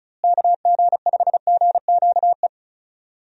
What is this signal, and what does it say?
Morse code, heard as a keyed tone, about 34 wpm, 700 Hz. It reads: KG5GQE
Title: USA Call Signs - 6 Characters - Repeated 3x - 35wpm